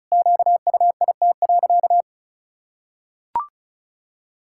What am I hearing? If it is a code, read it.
Morse code, heard as a keyed tone, about 36 wpm, 700 Hz. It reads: QUIT.
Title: Pangram Challenge v2 - 35wpm